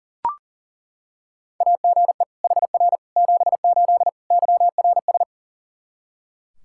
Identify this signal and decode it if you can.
Morse code, heard as a keyed tone, about 40 wpm, 700 Hz. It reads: AGE HR 78 YRS
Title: Practice QSO - Exchange 3 - More Info - 40wpm